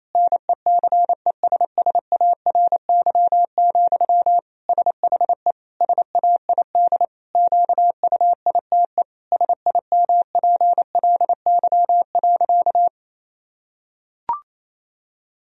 Morse code, heard as a keyed tone, about 28 wpm, 700 Hz. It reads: NECESSARY, SHE SAID QUITE SIMPLY.
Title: Sentences from Top 1000 Words - Review Part 4 - 28wpm